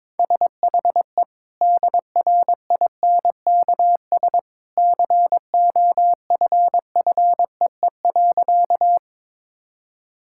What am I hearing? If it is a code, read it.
Morse code, heard as a keyed tone, about 22 wpm, 700 Hz. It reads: SHE DRINKS COFFEE.
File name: practice / 22wpm / Easy Sentences with Hints - 22wpm